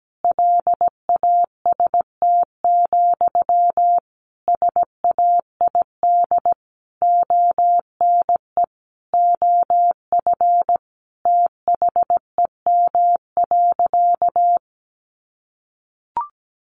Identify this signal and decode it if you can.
Morse code, heard as a keyed tone, about 17 wpm, 700 Hz. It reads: LAST, SAID ONE OF THEM.
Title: Sentences from Top 300 Words 17wpm